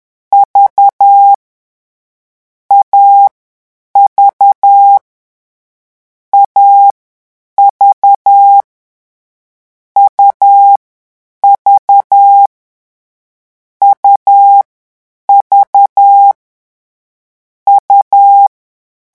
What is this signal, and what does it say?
Morse code, heard as a keyed tone, about 10 wpm, 800 Hz. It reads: V A V A V U V U V U
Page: Leçon 3 - vitesse 10 mots minute